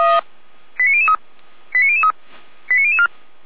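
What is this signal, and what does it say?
Five-tone selective call (ZVEI), heard as quick runs of five tones, tones of about 70 ms. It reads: 89003 89003 89004
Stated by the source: Five-tone ZVEI selective call received with an SDR.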